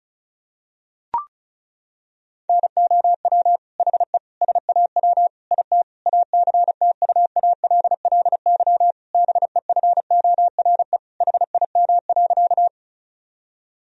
Morse code, heard as a keyed tone, about 36 wpm, 700 Hz. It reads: NOW HE SAW IT ACTUALLY BEFORE HIM.
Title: Sentences from Top 1000 Words - Review Part 4 - 35wpm